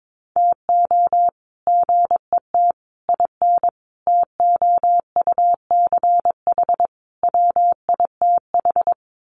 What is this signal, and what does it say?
Morse code, heard as a keyed tone, about 22 wpm, 700 Hz. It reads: TO GET IN TOUCH WITH
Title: Sets of 5 Words Encore2 22wpm